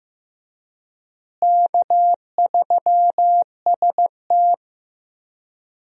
Morse code, heard as a keyed tone, about 15 wpm, 700 Hz. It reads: K3ST
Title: USA Call Signs - 4 Characters - Repeated 2x - 15wpm